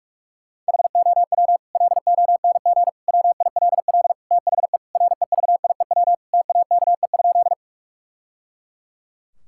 Morse code, heard as a keyed tone, about 41 wpm, 700 Hz. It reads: HOW LONG WILL THE REVIEW TAKE?
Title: Sentences from Top 1000 Words - Review Part 6 - 45wpm